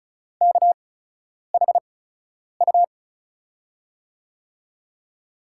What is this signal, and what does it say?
Morse code, heard as a keyed tone, about 36 wpm, spaced as for 10 wpm, 700 Hz. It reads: KHU
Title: USA Call Signs - 6 Characters - 8x Character Spacing - 35wpm